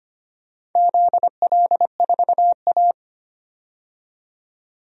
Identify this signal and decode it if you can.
Morse code, heard as a keyed tone, about 25 wpm, 700 Hz. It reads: ZL4A
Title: DXCC-100 Call Signs - 4 Characters - 25wpm